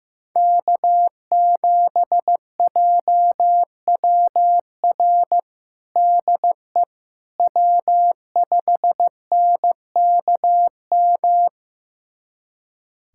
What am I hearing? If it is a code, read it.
Morse code, heard as a keyed tone, about 15 wpm, 700 Hz. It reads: K7JWR DE W5NKM
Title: USA Callsign DE Callsign 15wpm